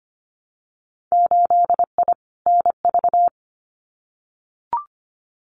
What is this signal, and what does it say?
Morse code, heard as a keyed tone, about 25 wpm, 700 Hz. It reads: 8I NV
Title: Winter Field Day Exchanges Encore 25wpm